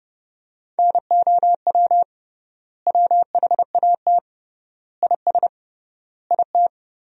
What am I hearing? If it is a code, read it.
Morse code, heard as a keyed tone, about 30 wpm, 700 Hz. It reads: NOW WHAT IS IT
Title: 3x Word-Spacing - Sentences from Top 200 Words - 30wpm